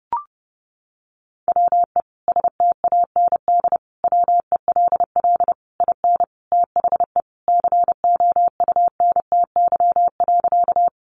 Morse code, heard as a keyed tone, about 30 wpm, 700 Hz. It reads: WE STAND WELL IN THE COUNTY.
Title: Sentences from Top 1200 Words 30wpm